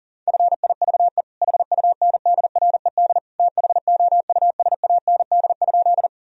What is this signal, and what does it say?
Morse code, heard as a keyed tone, about 40 wpm, 700 Hz. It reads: FIVE HUNDRED THOUSAND?
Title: Sentences from Top 1000 Words - Review Part 3 - 40wpm